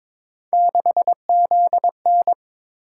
Morse code, heard as a keyed tone, about 22 wpm, 700 Hz. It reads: 6ZN
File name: practice / 22wpm / Three Letter-Number Combinations - 22wpm